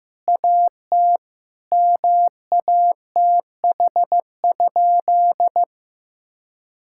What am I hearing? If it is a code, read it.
Morse code, heard as a keyed tone, about 15 wpm, 700 Hz. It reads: AT MATH?
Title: Sentences from Top 1000 Words - Review Part 6 - 15wpm